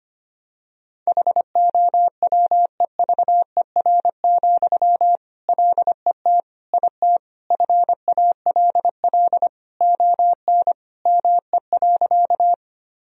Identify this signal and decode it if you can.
Morse code, heard as a keyed tone, about 25 wpm, 700 Hz. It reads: HOWEVER, LET IT FALL ON ME.
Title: Sentences from Top 1000 Words - Review Part 2 - 25wpm